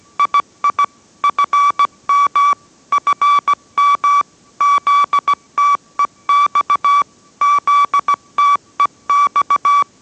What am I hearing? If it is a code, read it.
Morse code, heard as a keyed tone, about 17 wpm, 1150 Hz. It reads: II FM FM ZTEX ZTEX